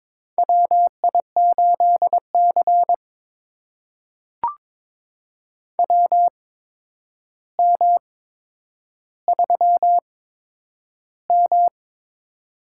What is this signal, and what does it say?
Morse code, heard as a keyed tone, about 22 wpm, spaced as for 9 wpm, 700 Hz. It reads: WI8C WM3M
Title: USA Call Signs - 4 Characters - 8x Character Spacing - 22wpm